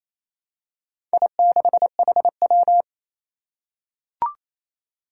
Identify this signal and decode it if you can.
Morse code, heard as a keyed tone, about 27 wpm, 700 Hz. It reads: I6HW
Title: Four Challenging Letter-Number Combinations 28wpm